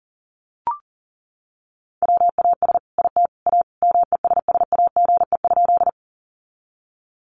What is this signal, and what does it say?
Morse code, heard as a keyed tone, about 40 wpm, 700 Hz. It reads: WAS IT A MESSAGE?
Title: Sentences from Top 1100 Words 40wpm